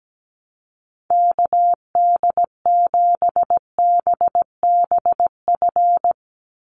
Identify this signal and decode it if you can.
Morse code, heard as a keyed tone, about 17 wpm, 700 Hz. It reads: KD7BBF
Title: USA Call Signs 17wpm